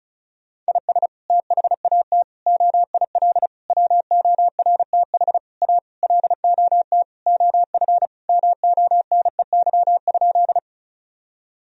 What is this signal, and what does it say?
Morse code, heard as a keyed tone, about 36 wpm, 700 Hz. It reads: IS THAT OIL WORTH A LOT OF MONEY?
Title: Sentences from Top 1000 Words - Review Part 4 - 35wpm